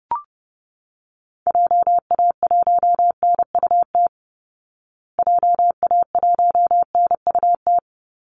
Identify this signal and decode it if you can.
Morse code, heard as a keyed tone, about 30 wpm, 700 Hz. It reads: JA1NUT JA1NUT
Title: Call Signs Repeated 2x - 30wpm